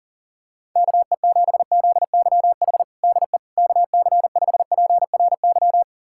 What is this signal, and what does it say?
Morse code, heard as a keyed tone, about 37 wpm, 700 Hz. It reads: KE7ZYH DE KC5PRY
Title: USA Callsign DE Callsign - 6 Characters - 40wpm